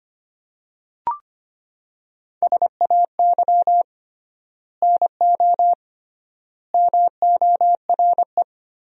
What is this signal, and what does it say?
Morse code, heard as a keyed tone, about 25 wpm, 700 Hz. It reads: SAY NO MORE
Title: 3x Word-Spacing - Sentences from Top 200 Words - 25wpm